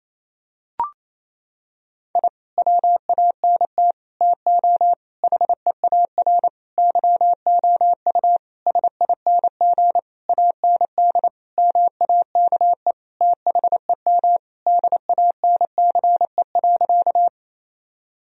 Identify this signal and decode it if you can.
Morse code, heard as a keyed tone, about 28 wpm, 700 Hz. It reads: I WANT TO HEAR YOU SING AND MAKE THEM DANCE.
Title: Sentences from Top 1000 Words - Review Part 4 - 28wpm